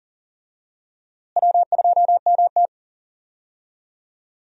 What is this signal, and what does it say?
Morse code, heard as a keyed tone, about 39 wpm, 700 Hz. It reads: W2MT
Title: USA Call Signs - 4 Characters - Repeated 2x - 40wpm